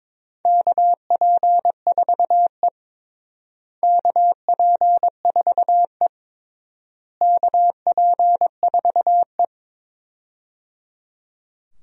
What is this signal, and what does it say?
Morse code, heard as a keyed tone, about 22 wpm, 700 Hz. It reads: KP4E KP4E KP4E
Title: DXCC-100 Call Signs - 4 Characters - Repeated 3x - 22wpm